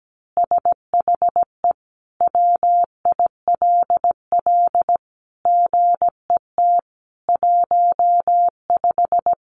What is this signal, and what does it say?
Morse code, heard as a keyed tone, about 17 wpm, 700 Hz. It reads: SHE WILL GET 15
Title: Easy Phrases 17wpm